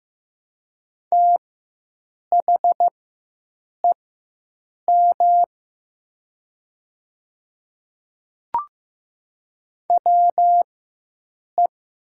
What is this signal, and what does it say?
Morse code, heard as a keyed tone, about 15 wpm, spaced as for 7 wpm, 700 Hz. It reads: THEM WE
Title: 4x Character Spacing - Top 100 Words - 15wpm